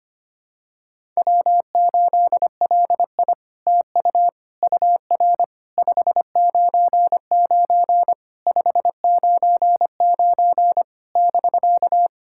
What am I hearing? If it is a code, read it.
Morse code, heard as a keyed tone, about 25 wpm, 700 Hz. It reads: W8LI TU UR 599 599 <BK>
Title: SOTA QSO Practice 25wpm